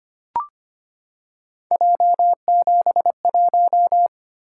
Morse code, heard as a keyed tone, about 25 wpm, 700 Hz. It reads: J71
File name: Three Letter-Number Combinations - 25wpm